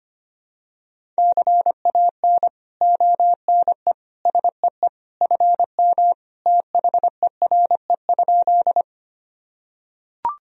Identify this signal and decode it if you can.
Morse code, heard as a keyed tone, about 25 wpm, 700 Hz. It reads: CAN ONE SEE FM THERE?
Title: Sentences from Top 100 Words and Abbreviations 25wpm